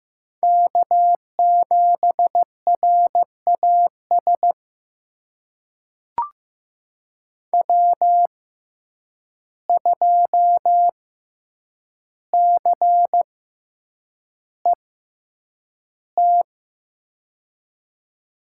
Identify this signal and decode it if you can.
Morse code, heard as a keyed tone, about 15 wpm, spaced as for 8 wpm, 700 Hz. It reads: K7RAS W2CET
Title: USA Call Signs - 6x Character Spacing - 15wpm